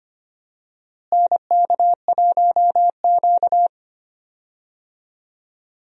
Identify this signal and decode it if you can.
Morse code, heard as a keyed tone, about 25 wpm, 700 Hz. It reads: NK1Q
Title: USA Call Signs - 4 Characters - Repeated 3x - 25wpm